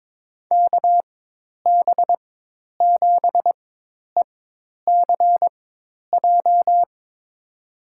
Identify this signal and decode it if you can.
Morse code, heard as a keyed tone, about 22 wpm, spaced as for 10 wpm, 700 Hz. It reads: KB7ECJ
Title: USA Call Signs - 4x Character Spacing - 22wpm